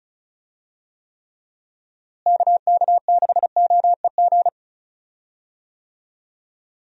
Morse code, heard as a keyed tone, about 34 wpm, 700 Hz. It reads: KK6OEG
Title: USA Call Signs - 6 Characters - Repeated 3x - 35wpm